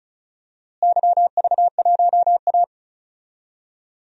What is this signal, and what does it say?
Morse code, heard as a keyed tone, about 34 wpm, 700 Hz. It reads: YV1A
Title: DXCC-100 Call Signs - 4 Characters - Repeated 3x - 35wpm